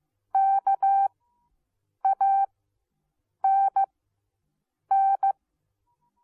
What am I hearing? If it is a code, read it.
Morse code, heard as a keyed tone, about 15 wpm, spaced as for 7 wpm, 780 Hz. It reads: KANN